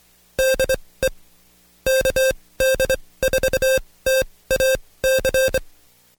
Morse code, heard as a keyed tone, about 24 wpm, spaced as for 18 wpm, 530 Hz. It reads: DE KD4TAC
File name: Code Quick / AudioDownld2 / Seg135-Stepper8 17WPM